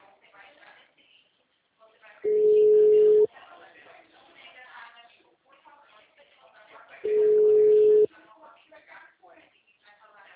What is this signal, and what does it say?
Ringback tone (Europe, 425 Hz), with speech.